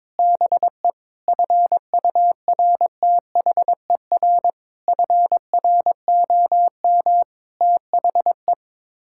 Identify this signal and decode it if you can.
Morse code, heard as a keyed tone, about 22 wpm, 700 Hz. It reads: BE FURTHER FROM THE
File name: practice / 22wpm / Sets of 6 Words - 22wpm